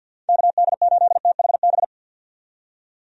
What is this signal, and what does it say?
Morse code, heard as a keyed tone, about 53 wpm, 700 Hz. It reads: KD8THB
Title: USA Call Signs - 6 Characters - 50wpm